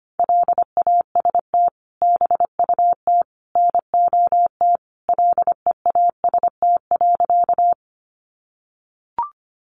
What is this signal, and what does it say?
Morse code, heard as a keyed tone, about 25 wpm, 700 Hz. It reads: LAST BUT NOT LEAST.
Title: Sentences from Top 700 Words 25wpm